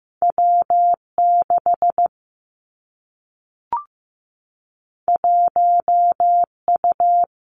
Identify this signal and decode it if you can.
Morse code, heard as a keyed tone, about 15 wpm, 700 Hz. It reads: W 6 1 U
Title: Two Challenging Letter-Number Combinations 15wpm